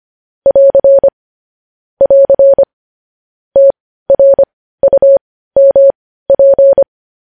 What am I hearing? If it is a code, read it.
Morse code, heard as a keyed tone, about 25 wpm, spaced as for 15 wpm, 550 Hz. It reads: <AR> <AR> TRUMP
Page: news-headlines-in-morse-code-at-15-wpm